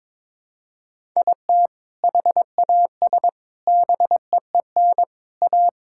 Morse code, heard as a keyed tone, about 22 wpm, 700 Hz. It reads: IT HAS BEEN A
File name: practice / 22wpm / Sentences with 4-Word Phrases - 22wpm